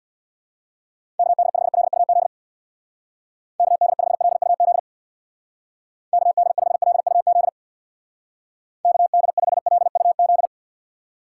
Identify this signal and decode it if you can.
Morse code, heard as a keyed tone, about 62 wpm, 700 Hz. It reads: KD5LUZ KD5LUZ KD5LUZ KD5LUZ
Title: Speed Racing - USA 2x3 Call Signs 50wpm